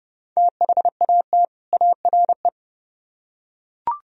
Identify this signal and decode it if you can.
Morse code, heard as a keyed tone, about 30 wpm, 700 Hz. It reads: THAT ARE